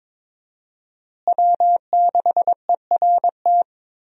Morse code, heard as a keyed tone, about 22 wpm, 700 Hz. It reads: W6ERT
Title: USA Call Signs - 5 Characters - Repeated 2x - 22wpm